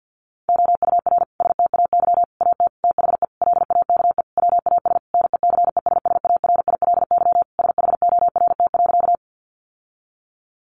Morse code, heard as a keyed tone, about 50 wpm, 700 Hz. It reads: OUR STAY AT THE LAKE WAS NECESSARILY SHORT.
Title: Sentences from Top 1400 Words 50wpm